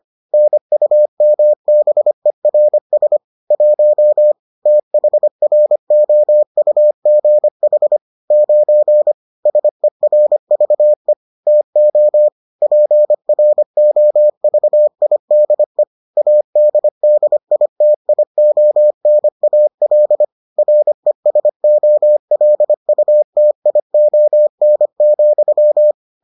Morse code, heard as a keyed tone, about 25 wpm, 600 Hz. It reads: NUMBERS 1 THROUGH 9 SERVE TO PROVIDE ADDITIONAL RESOLUTION,